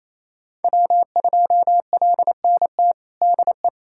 Morse code, heard as a keyed tone, about 28 wpm, 700 Hz. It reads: W2LNT DE
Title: USA Callsign DE Callsign - 5 Characters - 28wpm